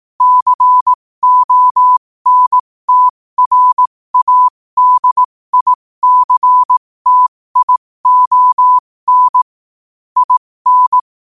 Morse code, heard as a keyed tone, about 18 wpm, spaced as for 15 wpm, 1000 Hz. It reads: CONTRADICTION IN